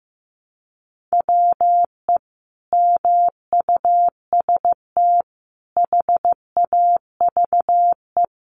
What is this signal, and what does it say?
Morse code, heard as a keyed tone, about 15 wpm, 700 Hz. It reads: WE MUST HAVE
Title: Sentences from Top 1200 Words 15wpm